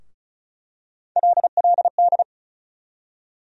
Morse code, heard as a keyed tone, about 34 wpm, 700 Hz. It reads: LLD
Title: USA Call Sign Postfixes - 3 Letters 35wpm